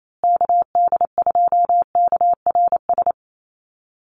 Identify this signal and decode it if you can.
Morse code, heard as a keyed tone, about 28 wpm, 700 Hz. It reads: KD2KRS
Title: USA Call Signs Repeated 2x 28wpm